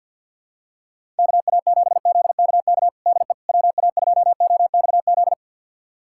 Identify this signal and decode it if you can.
Morse code, heard as a keyed tone, about 46 wpm, 700 Hz. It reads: KA7ZKK DE WA2OXZ
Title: USA Callsign DE Callsign - 6 Characters - 50wpm